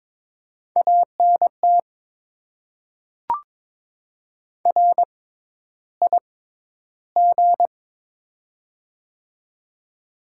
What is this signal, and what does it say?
Morse code, heard as a keyed tone, about 22 wpm, spaced as for 11 wpm, 690 Hz. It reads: ANT RIG
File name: Top 20 QSO Elements - 6x Character Spacing - 22wpm